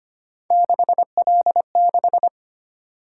Morse code, heard as a keyed tone, about 29 wpm, 700 Hz. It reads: 6L6
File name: Speed-Racing - Three Letter-Number Combinations - 25wpm